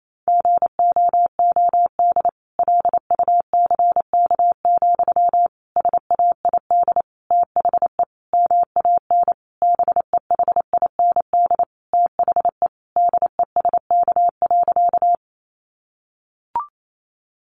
Morse code, heard as a keyed tone, about 28 wpm, 700 Hz. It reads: GOOD LUCK, SAID THE MAN BEHIND THE DESK.
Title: A Sound Of Thunder 28wpm